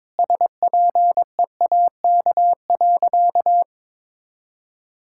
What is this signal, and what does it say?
Morse code, heard as a keyed tone, about 22 wpm, 700 Hz. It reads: SPEAK.
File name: practice / 22wpm / Sentences with 3-Word Phrases - 22wpm